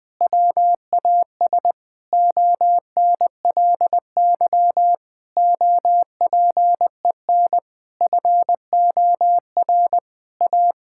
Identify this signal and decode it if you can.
Morse code, heard as a keyed tone, about 20 wpm, 700 Hz. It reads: WAS ONLY OPEN FOR A
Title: Sentences from Top 1600 Words 20wpm